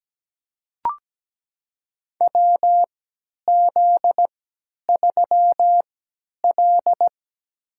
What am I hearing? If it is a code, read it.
Morse code, heard as a keyed tone, about 17 wpm, spaced as for 10 wpm, 700 Hz. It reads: WZ3L
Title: USA Call Signs - 3x Character Spacing - 17wpm